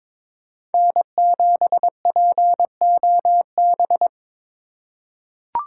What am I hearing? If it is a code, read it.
Morse code, heard as a keyed tone, about 21 wpm, 700 Hz. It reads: N7POB
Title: USA Call Signs - 8x Character Spacing - 22wpm